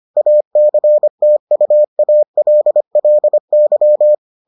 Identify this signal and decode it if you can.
Morse code, heard as a keyed tone, about 25 wpm, 600 Hz. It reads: ACTUALLY